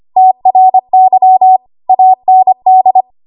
Morse code, heard as a keyed tone, about 25 wpm, 750 Hz. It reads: TRY AND
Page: 25 WPM Code Archive